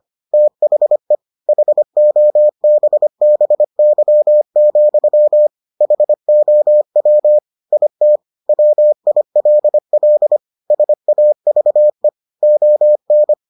Morse code, heard as a keyed tone, about 25 wpm, 600 Hz. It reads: THE HOBBY, HOW IT WILL SAVE ON